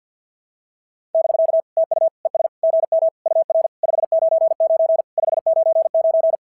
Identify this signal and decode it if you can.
Morse code, heard as a keyed tone, about 53 wpm, 650 Hz. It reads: <BK> TU ES GM UR 599 599
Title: SOTA QSO Practice 50wpm